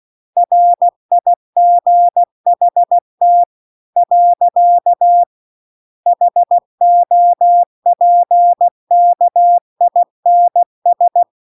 Morse code, heard as a keyed tone, about 16 wpm, 700 Hz. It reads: RIGHT . HOPKINS